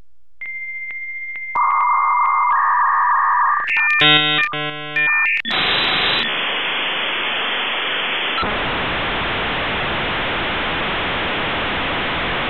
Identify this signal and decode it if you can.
Dial-up modem handshake.